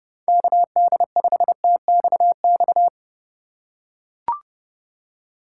Morse code, heard as a keyed tone, about 30 wpm, 700 Hz. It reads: KD5TXX
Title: K1USN SST QSO Practice - 30wpm